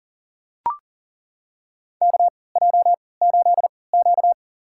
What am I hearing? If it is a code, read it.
Morse code, heard as a keyed tone, about 40 wpm, spaced as for 23 wpm, 700 Hz. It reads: KJ8Q
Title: USA Call Signs - 3x Character Spacing - 40wpm